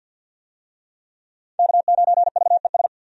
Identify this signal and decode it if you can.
Morse code, heard as a keyed tone, about 51 wpm, 700 Hz. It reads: K9VES
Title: USA Call Signs - 5 Characters - 50wpm